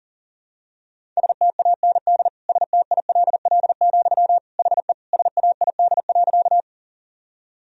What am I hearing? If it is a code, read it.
Morse code, heard as a keyed tone, about 37 wpm, 700 Hz. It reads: STAND STILL, HE SAID.
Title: Sentences from Top 1000 Words - Review Part 1 - 40wpm